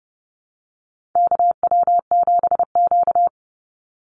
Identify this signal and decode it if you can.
Morse code, heard as a keyed tone, about 30 wpm, 700 Hz. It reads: KW7Q
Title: Call Signs Repeated 3x - 30wpm